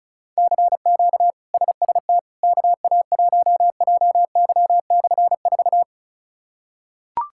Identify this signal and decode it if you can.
Morse code, heard as a keyed tone, about 34 wpm, 700 Hz. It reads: CQ SST KA1JY/4
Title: K1USN SST QSO Practice - 35wpm